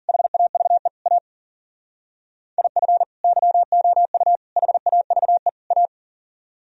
Morse code, heard as a keyed tone, about 52 wpm, 700 Hz. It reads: HAVE A IF YOU HAVE A